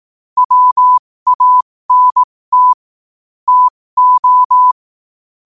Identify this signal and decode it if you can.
Morse code, heard as a keyed tone, about 18 wpm, spaced as for 15 wpm, 1000 Hz. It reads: WANT TO
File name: qotdfiles / 15wpm